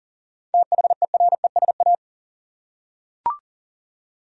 Morse code, heard as a keyed tone, about 40 wpm, 700 Hz. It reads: THERESA
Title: USA Common Names - 40wpm